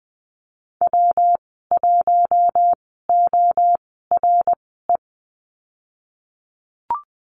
Morse code, heard as a keyed tone, about 20 wpm, spaced as for 15 wpm, 700 Hz. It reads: W1ORE
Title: 173 Lesson 28g - Number 9 - Call Signs 20wpm